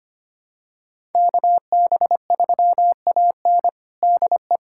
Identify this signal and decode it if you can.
Morse code, heard as a keyed tone, about 25 wpm, 700 Hz. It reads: KB3AN DE
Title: USA Callsign DE Callsign - 5 Characters - 25wpm